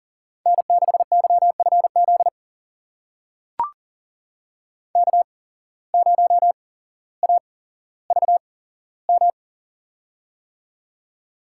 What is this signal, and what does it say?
Morse code, heard as a keyed tone, about 40 wpm, spaced as for 17 wpm, 700 Hz. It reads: N6YFZ K0AVM
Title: USA Call Signs - 5 Characters - 8x Character Spacing - 40wpm